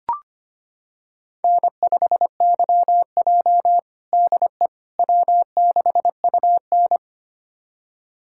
Morse code, heard as a keyed tone, about 25 wpm, 700 Hz. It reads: N5YJ DE W6UN
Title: USA Callsign DE Callsign - 4 Characters - 25wpm